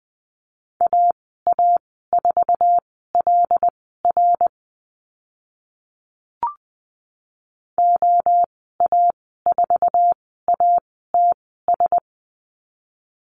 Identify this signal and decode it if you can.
Morse code, heard as a keyed tone, about 20 wpm, spaced as for 14 wpm, 700 Hz. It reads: AA4LR OA4ATS
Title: 082 Lesson 14g - Number 2 - Call Signs 20wpm